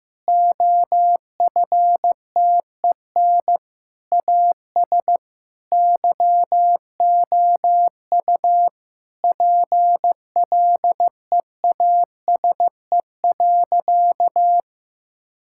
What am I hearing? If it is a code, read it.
Morse code, heard as a keyed tone, about 15 wpm, 700 Hz. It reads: OFTEN AS YOU PLEASE.
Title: Sentences from Top 1600 Words 15wpm